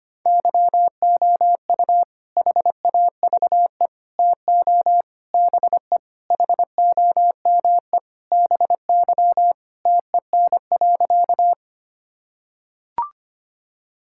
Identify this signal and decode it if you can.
Morse code, heard as a keyed tone, about 25 wpm, 700 Hz. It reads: YOU HAVE TO BE HOME BY TEN.
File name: practice / 25wpm / Sentences with 4-Word Phrases - 25wpm